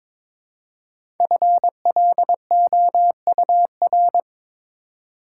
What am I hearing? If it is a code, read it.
Morse code, heard as a keyed tone, about 22 wpm, 690 Hz. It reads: FLOUR
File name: 5 Letter Words - First 3 Letters Given - 22wpm